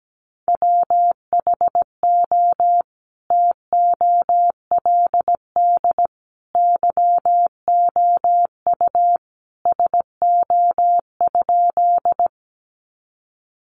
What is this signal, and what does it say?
Morse code, heard as a keyed tone, about 17 wpm, 700 Hz. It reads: WHO TOLD YOU SO?
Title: Sentences from Top 400 Words 17wpm